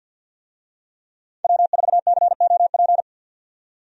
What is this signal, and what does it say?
Morse code, heard as a keyed tone, about 53 wpm, 700 Hz. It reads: W4COP
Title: USA Call Signs - 5 Characters - 50wpm